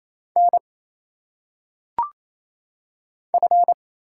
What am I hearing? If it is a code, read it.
Morse code, heard as a keyed tone, about 27 wpm, 700 Hz. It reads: N F
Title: Single Letters - Phonetic Pronunciation 28wpm